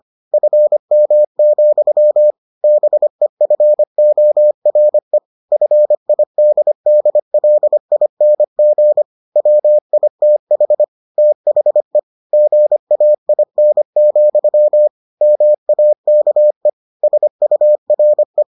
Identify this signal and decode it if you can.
Morse code, heard as a keyed tone, about 25 wpm, 600 Hz. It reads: FM, BEFORE FIDDLING WITH THE GAIN, MAKE SURE